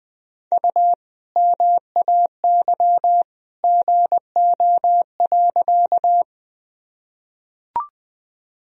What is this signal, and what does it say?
Morse code, heard as a keyed tone, about 20 wpm, 700 Hz. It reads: U MAY GO.
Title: Sentences from Top 100 Words and Abbreviations 20wpm